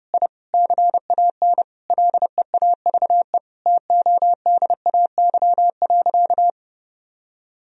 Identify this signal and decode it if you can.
Morse code, heard as a keyed tone, about 30 wpm, 700 Hz. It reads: I CAN LEAVE TODAY.
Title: Sentences from Top 1000 Words - Review Part 3 - 30wpm